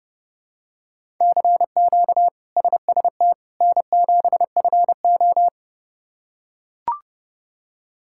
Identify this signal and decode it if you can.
Morse code, heard as a keyed tone, about 31 wpm, 700 Hz. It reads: CQ SST N7FO
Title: K1USN SST QSO Practice - 30wpm